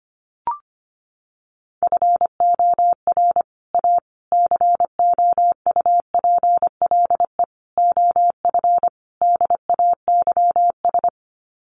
Morse code, heard as a keyed tone, about 25 wpm, 700 Hz. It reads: FOR A COUPLE OF DAYS
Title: Sets of 5 Words Encore2 25wpm